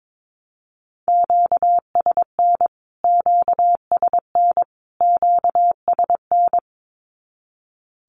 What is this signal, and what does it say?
Morse code, heard as a keyed tone, about 22 wpm, 700 Hz. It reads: QSN QSN QSN
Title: Q-Codes Repeated 3x 22wpm